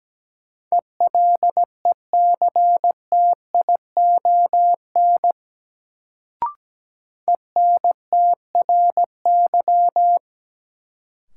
Morse code, heard as a keyed tone, about 17 wpm, 700 Hz. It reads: ELECTION ENTRY
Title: Sentences from Top 1500 Words - New Word Drill - 17wpm